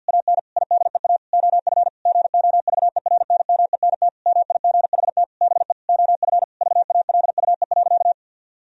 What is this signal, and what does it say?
Morse code, heard as a keyed tone, about 50 wpm, 700 Hz. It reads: AN IDEA OF GOVERNMENT MIGHT BE OF VALUE.
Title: Sentences from Top 1000 Words - Review Part 3 - 50wpm